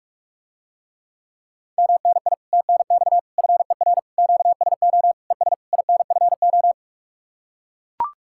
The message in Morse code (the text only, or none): MNI TNX FER QSO ES INFO